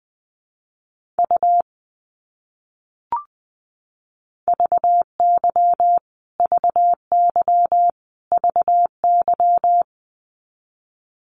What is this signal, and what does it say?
Morse code, heard as a keyed tone, about 20 wpm, 700 Hz. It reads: U VY VY VY
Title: Abbreviations Repeated 3x - 20wpm